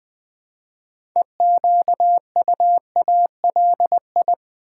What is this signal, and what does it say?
Morse code, heard as a keyed tone, about 20 wpm, 700 Hz. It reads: EQUALI
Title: Build A Word - 2 Letters At A Time - 8-Letter Words - 20wpm